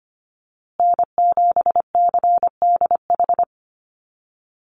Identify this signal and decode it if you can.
Morse code, heard as a keyed tone, about 25 wpm, 700 Hz. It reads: N7CDH
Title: USA Call Signs Repeated 3x 25wpm